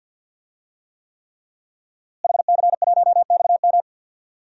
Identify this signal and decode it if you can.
Morse code, heard as a keyed tone, about 45 wpm, 700 Hz. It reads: HC1XM
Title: DXCC-100 Call Signs - 5 Characters - Repeated 3x - 50wpm